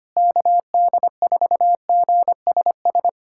Morse code, heard as a keyed tone, about 25 wpm, 700 Hz. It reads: KD4GSS
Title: USA Callsign DE Callsign - 6 Characters - 25wpm